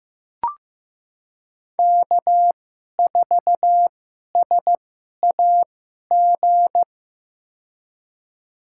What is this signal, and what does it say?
Morse code, heard as a keyed tone, about 16 wpm, spaced as for 11 wpm, 700 Hz. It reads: K4SAG
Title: USA Call Signs - 5 Characters - 2x Character Spacing - 15wpm